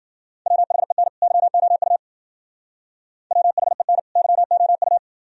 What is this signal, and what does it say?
Morse code, heard as a keyed tone, about 51 wpm, 700 Hz. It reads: WHEN YOU WHEN YOU